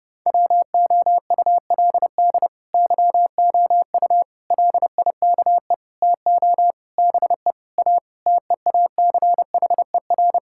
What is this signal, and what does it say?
Morse code, heard as a keyed tone, about 30 wpm, 700 Hz. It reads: WOULD YOU LIKE TO BE A TEACHER
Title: Sentences from Top 1000 Words - Review Part 6 - 30wpm